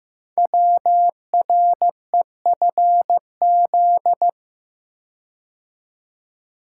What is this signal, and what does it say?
Morse code, heard as a keyed tone, about 15 wpm, 700 Hz. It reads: WREFZ
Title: Code Groups - 5 Letter:Numbers - 15wpm